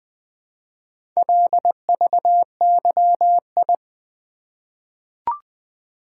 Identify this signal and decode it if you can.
Morse code, heard as a keyed tone, about 20 wpm, 700 Hz. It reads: LVYI
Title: Four Letter-Number Combinations - 20wpm